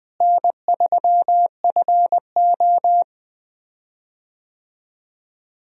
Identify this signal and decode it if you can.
Morse code, heard as a keyed tone, about 20 wpm, 700 Hz. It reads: N3FO
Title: USA Call Signs - 4 Characters - Repeated 2x - 20wpm